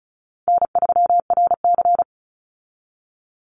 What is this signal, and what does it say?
Morse code, heard as a keyed tone, about 34 wpm, 700 Hz. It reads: N3RC
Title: Easy Call Signs Repeated 2x - 35wpm